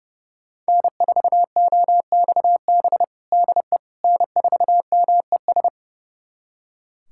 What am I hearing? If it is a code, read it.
Morse code, heard as a keyed tone, about 31 wpm, 700 Hz. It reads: N4OXB DE N4MES
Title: USA Callsign DE Callsign - 5 Characters - 30wpm